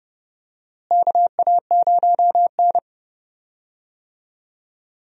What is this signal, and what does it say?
Morse code, heard as a keyed tone, about 29 wpm, 700 Hz. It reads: KA0N
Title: USA Call Signs - 4 Characters - Repeated 3x - 30wpm